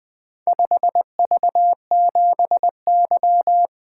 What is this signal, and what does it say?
Morse code, heard as a keyed tone, about 20 wpm, 700 Hz. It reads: 5V7Y
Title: Four Letter-Number Combinations - 20wpm